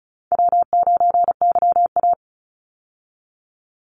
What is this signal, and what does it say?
Morse code, heard as a keyed tone, about 34 wpm, 700 Hz. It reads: W9YA
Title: Easy Call Signs Repeated 3x - 35wpm